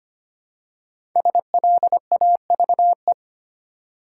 Morse code, heard as a keyed tone, about 25 wpm, 700 Hz. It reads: SLAVE
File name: 5 Letter Words - First 3 Letters Given - 25wpm